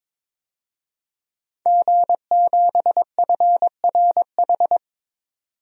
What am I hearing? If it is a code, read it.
Morse code, heard as a keyed tone, about 21 wpm, 700 Hz. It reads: G7FRH
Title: DXCC-100 Call Signs - 5 Characters - 22wpm